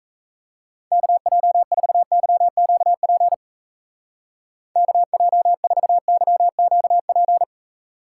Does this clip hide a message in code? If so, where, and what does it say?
Yes: Morse code from 0.9 to 7.4 s, heard as a keyed tone, about 36 wpm, 700 Hz. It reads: KJ4YQP KJ4YQP